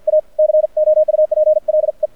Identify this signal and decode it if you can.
Morse code, heard as a keyed tone, about 50 wpm, 620 Hz. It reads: A COUPLE
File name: archived / CRSnet